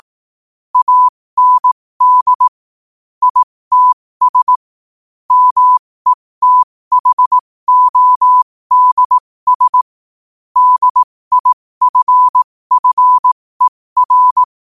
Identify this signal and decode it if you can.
Morse code, heard as a keyed tone, about 18 wpm, spaced as for 15 wpm, 1000 Hz. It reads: AND ITS METHODS DIFFER
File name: qotdfiles / 15wpm